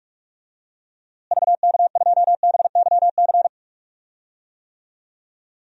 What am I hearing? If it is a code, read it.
Morse code, heard as a keyed tone, about 46 wpm, 700 Hz. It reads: VK2BYC
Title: DXCC-100 Call Signs - 6 Characters - Repeated 3x - 45wpm